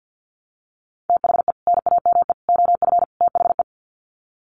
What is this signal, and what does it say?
Morse code, heard as a keyed tone, about 50 wpm, 700 Hz. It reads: THE NAME OF THE
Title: Sets of 4 Words 50wpm